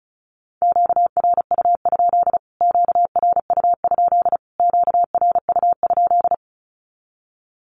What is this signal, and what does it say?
Morse code, heard as a keyed tone, about 34 wpm, 700 Hz. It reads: QRU? QRU? QRU?
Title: Q-Codes Repeated 3x 35wpm